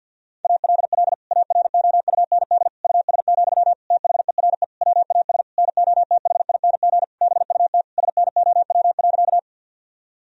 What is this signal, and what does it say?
Morse code, heard as a keyed tone, about 50 wpm, 700 Hz. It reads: ALL AROUND US, THERE WAS NOTHING BUT SNOW.
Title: Sentences from Top 1000 Words - Review Part 3 - 50wpm